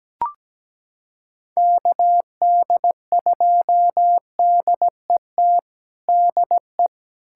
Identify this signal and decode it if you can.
Morse code, heard as a keyed tone, about 17 wpm, 700 Hz. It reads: KD2DET DE
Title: USA Callsign DE Callsign - 6 Characters - 17wpm